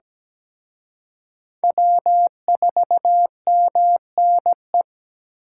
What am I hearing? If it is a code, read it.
Morse code, heard as a keyed tone, about 20 wpm, 700 Hz. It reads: W4MNE
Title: Speed Racing - USA 1x3 Call Signs 17wpm